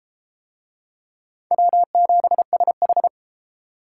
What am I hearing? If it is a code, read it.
Morse code, heard as a keyed tone, about 33 wpm, 700 Hz. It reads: W7SH